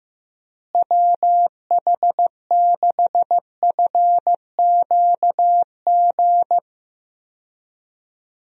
Morse code, heard as a keyed tone, about 15 wpm, 700 Hz. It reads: WH6FQG
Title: USA Call Signs - 6 Characters - Repeated 3x - 15wpm